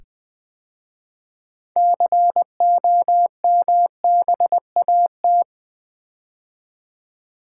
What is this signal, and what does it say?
Morse code, heard as a keyed tone, about 20 wpm, 700 Hz. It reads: COMBAT
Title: 6 Letter Words - First 3 Letters Given - 20wpm